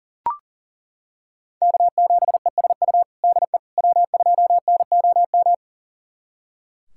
Morse code, heard as a keyed tone, about 40 wpm, 700 Hz. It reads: K7ESU DE W2NOM
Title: USA Callsign DE Callsign - 5 Characters - 40wpm